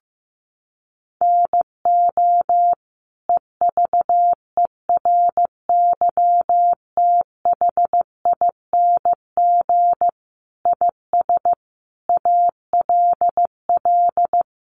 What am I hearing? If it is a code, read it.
Morse code, heard as a keyed tone, about 15 wpm, 700 Hz. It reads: NO EVERYTHING IS ALL
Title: Sentences from Top 700 Words 15wpm